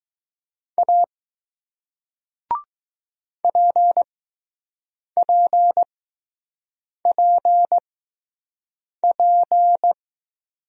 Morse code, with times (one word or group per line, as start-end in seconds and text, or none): 0.8-1.0 A
3.4-9.9 PPPP